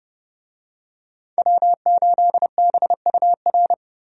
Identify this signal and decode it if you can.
Morse code, heard as a keyed tone, about 31 wpm, 700 Hz. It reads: W8BUR
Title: USA Call Signs - 5 Characters - 30wpm